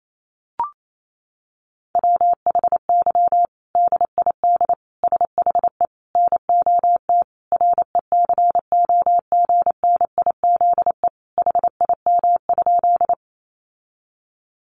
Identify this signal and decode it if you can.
Morse code, heard as a keyed tone, about 28 wpm, 700 Hz. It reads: WHY DID SHE NOT RECOGNIZE HIM?
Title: Sentences from Top 1400 Words 28wpm